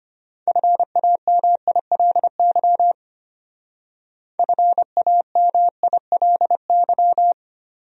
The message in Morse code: FAMILY FAMILY

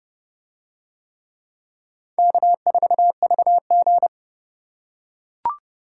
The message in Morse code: K4VG

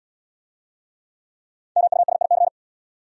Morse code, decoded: MUSIC

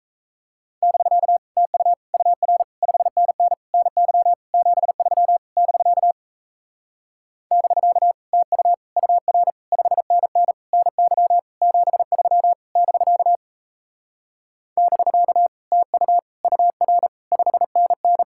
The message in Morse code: <BK> TU UR 5NN NY 73 <BK> <BK> TU UR 5NN NY 73 <BK> <BK> TU UR 5NN

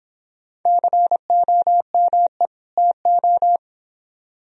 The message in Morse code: COME TO

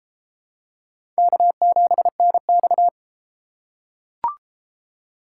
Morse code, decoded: K7NX

